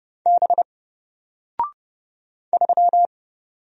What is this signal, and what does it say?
Morse code, heard as a keyed tone, about 31 wpm, 700 Hz. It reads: B 3
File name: Single-Letter-Number-30wpm